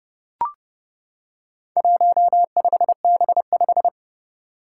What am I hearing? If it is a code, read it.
Morse code, heard as a keyed tone, about 30 wpm, 690 Hz. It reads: 15B5
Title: Four Challenging Letter-Number Combinations 30wpm